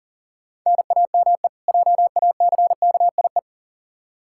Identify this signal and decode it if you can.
Morse code, heard as a keyed tone, about 40 wpm, 700 Hz. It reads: NAME JACKIE
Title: K1USN SST QSO Practice 40wpm